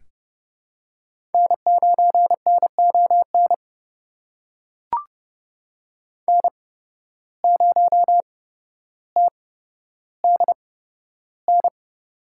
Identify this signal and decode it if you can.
Morse code, heard as a keyed tone, about 30 wpm, spaced as for 13 wpm, 700 Hz. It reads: N9NON N0TDN
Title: USA Call Signs - 5 Characters - 8x Character Spacing - 30wpm